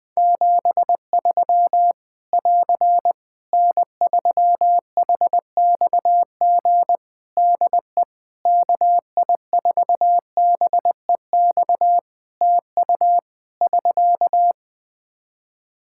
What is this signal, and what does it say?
Morse code, heard as a keyed tone, about 20 wpm, 700 Hz. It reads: 73 <AR> N3HXG DE KI4BEX TU <SK>
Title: K1USN SST QSO Practice 20wpm